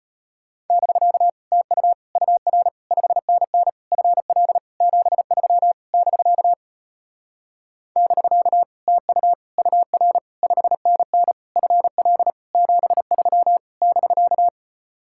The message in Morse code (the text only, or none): <BK> TU UR 5NN FL 73 <BK> <BK> TU UR 5NN FL 73 <BK>